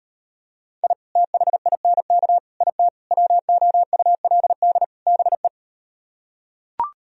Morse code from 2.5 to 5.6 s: IT WOULD BE

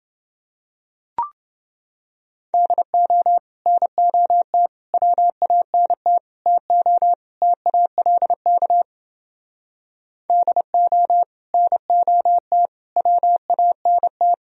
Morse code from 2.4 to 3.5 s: DO

